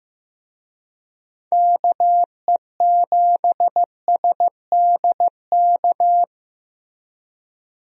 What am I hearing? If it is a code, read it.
Morse code, heard as a keyed tone, about 15 wpm, 700 Hz. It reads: KE7SDK
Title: USA Call Signs - 6 Characters - Repeated 2x - 15wpm